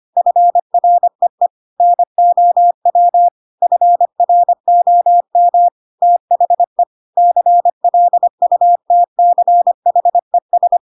Morse code, heard as a keyed tone, about 25 wpm, 700 Hz. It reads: FREE NOW FROM THE CLUTCHES
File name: Pinocchio / Pinocchio ch4 25 25 ik1hge